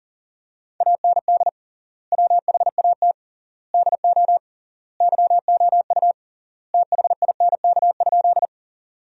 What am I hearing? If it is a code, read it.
Morse code, heard as a keyed tone, about 40 wpm, 700 Hz. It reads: AND WHAT DO YOU THINK?
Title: 3x Word-Spacing - Sentences from Top 200 Words - 40wpm